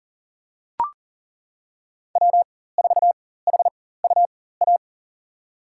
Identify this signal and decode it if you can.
Morse code, heard as a keyed tone, about 40 wpm, spaced as for 19 wpm, 700 Hz. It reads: W4HUA
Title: USA Call Signs - 5 Characters - 4x Character Spacing - 40wpm